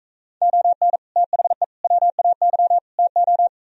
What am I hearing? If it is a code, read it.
Morse code, heard as a keyed tone, about 42 wpm, 700 Hz. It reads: ON THE WAY TO